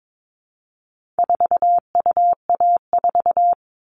Morse code, heard as a keyed tone, about 22 wpm, 700 Hz. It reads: 4UA4
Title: All Cut Numbers - 4-Digit 22wpm